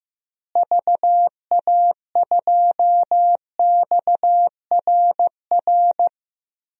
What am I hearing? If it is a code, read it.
Morse code, heard as a keyed tone, about 16 wpm, 700 Hz. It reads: VA2XRR
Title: DXCC-100 Call Signs - 6 Characters - Repeated 3x - 15wpm